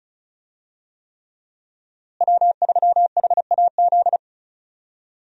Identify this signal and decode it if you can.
Morse code, heard as a keyed tone, about 36 wpm, 700 Hz. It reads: W3HAZ
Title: USA Call Signs - 5 Characters - Repeated 3x - 35wpm